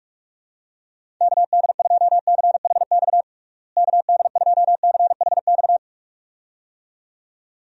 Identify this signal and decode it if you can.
Morse code, heard as a keyed tone, about 46 wpm, 700 Hz. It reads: KD2CHX KD2CHX
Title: USA Call Signs - 6 Characters - Repeated 2x - 45wpm